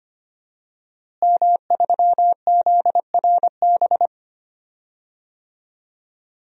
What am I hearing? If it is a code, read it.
Morse code, heard as a keyed tone, about 25 wpm, 700 Hz. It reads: M3ZRB
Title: DXCC-100 Call Signs - 5 Characters - Repeated 3x - 25wpm